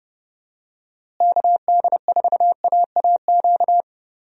Morse code, heard as a keyed tone, about 31 wpm, 700 Hz. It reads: KD4AAQ
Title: USA Call Signs - 6 Characters - Repeated 3x - 30wpm